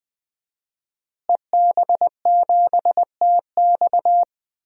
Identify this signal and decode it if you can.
Morse code, heard as a keyed tone, about 20 wpm, 700 Hz. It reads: EB7TX
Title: DXCC-100 Call Signs - 5 Characters - Repeated 2x - 20wpm